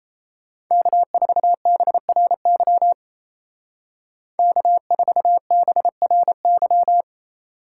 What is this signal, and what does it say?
Morse code, heard as a keyed tone, about 32 wpm, 700 Hz. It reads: K4BRY K4BRY
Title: Speed Racing - USA Call Signs 28wpm